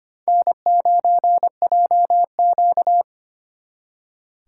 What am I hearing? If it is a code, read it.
Morse code, heard as a keyed tone, about 32 wpm, 700 Hz. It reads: N9JQ